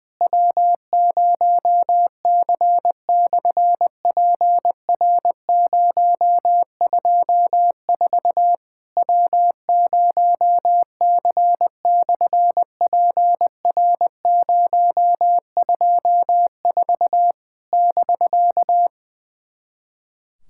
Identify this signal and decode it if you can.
Morse code, heard as a keyed tone, about 20 wpm, 700 Hz. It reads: W0C/PR024 W0C/PR024 <BK>
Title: SOTA QSO Practice 20wpm